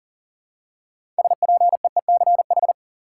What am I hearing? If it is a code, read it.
Morse code, heard as a keyed tone, about 40 wpm, 700 Hz. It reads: SPEECH
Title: Six Letter Words - 40wpm